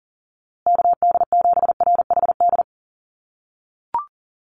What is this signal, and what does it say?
Morse code, heard as a keyed tone, about 40 wpm, 700 Hz. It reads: KD7RHD
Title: USA Call Signs Repeated 3x 40wpm